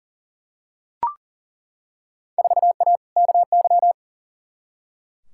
Morse code, heard as a keyed tone, about 40 wpm, 700 Hz. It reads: 4A KY
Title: Field Day Exchanges Encore - 40wpm